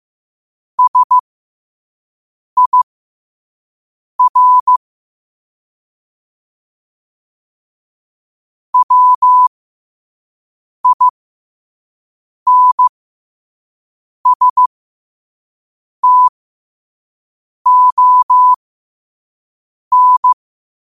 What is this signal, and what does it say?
Morse code, heard as a keyed tone, about 15 wpm, spaced as for 5 wpm, 1000 Hz. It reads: SIR WINSTON
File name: qotdfiles / 5wpm